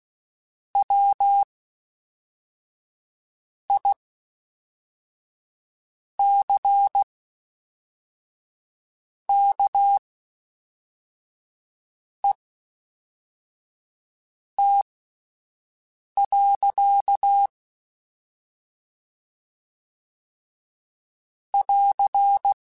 Morse code, heard as a keyed tone, about 16 wpm, spaced as for 4 wpm, 800 Hz. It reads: WICKET. <AR>
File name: Misc / Misc-mp3s / WordPairs sp @5